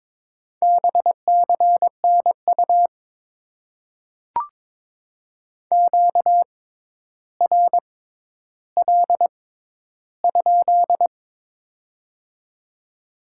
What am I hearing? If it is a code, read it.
Morse code, heard as a keyed tone, about 22 wpm, spaced as for 11 wpm, 700 Hz. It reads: BCNU QRL?
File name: Top 90 QSO Elements - 6x Character Spacing - 22wpm